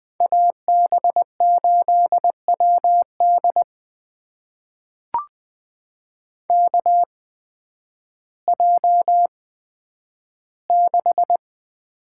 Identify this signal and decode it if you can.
Morse code, heard as a keyed tone, about 20 wpm, spaced as for 11 wpm, 700 Hz. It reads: AB8WD KJ6
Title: USA Call Signs - 8x Character Spacing - 20wpm